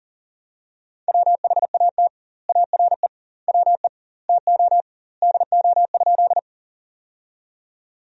Morse code, heard as a keyed tone, about 40 wpm, 700 Hz. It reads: WHAT ARE WE TO DO?
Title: 2x Word-Spacing - Sentences from Top 100 Words - 40wpm